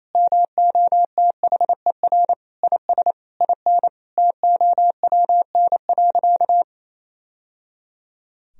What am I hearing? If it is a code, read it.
Morse code, heard as a keyed tone, about 28 wpm, 700 Hz. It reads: MOTHER IS IN TOWN.
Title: Sentences from Top 1000 Words - Review Part 2 - 28wpm